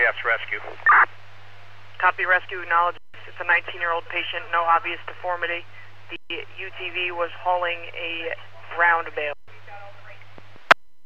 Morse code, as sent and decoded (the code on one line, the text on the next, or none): none